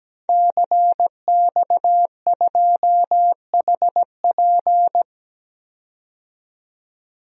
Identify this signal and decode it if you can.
Morse code, heard as a keyed tone, about 17 wpm, 700 Hz. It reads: CX2HP
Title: DXCC-100 Call Signs - 5 Characters - 17wpm